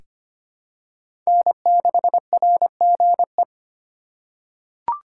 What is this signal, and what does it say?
Morse code, heard as a keyed tone, about 25 wpm, 700 Hz. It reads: N6RGE
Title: USA Call Signs - 5 Characters - 4x Character Spacing - 25wpm